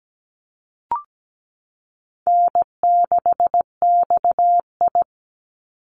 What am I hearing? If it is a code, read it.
Morse code, heard as a keyed tone, about 17 wpm, 700 Hz. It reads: N6XI
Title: Easy Call Signs Repeated 3x - 17wpm